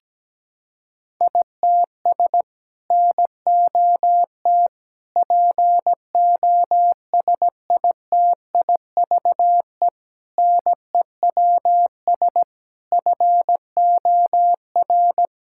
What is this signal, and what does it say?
Morse code, heard as a keyed tone, about 17 wpm, 700 Hz. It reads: ITS NOT POSITIVE NEWS FOR
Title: Sentences from Top 1000 Words - Review Part 5 - 17wpm